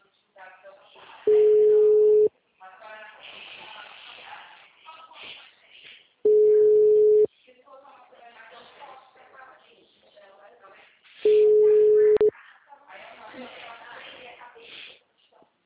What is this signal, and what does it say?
Ringback tone (Europe, 425 Hz), with speech.